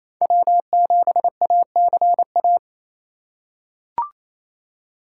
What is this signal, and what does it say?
Morse code, heard as a keyed tone, about 28 wpm, 700 Hz. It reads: W7ACA
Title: USA Call Signs - 4x Character Spacing - 28wpm